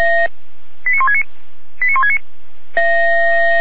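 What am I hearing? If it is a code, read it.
Five-tone selective call (ZVEI), heard as quick runs of five tones, tones of about 70 ms. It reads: 89169 89169
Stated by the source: ZVEI five-tone selective call received on SDR, fire brigade Feuerwehr Orlamünde.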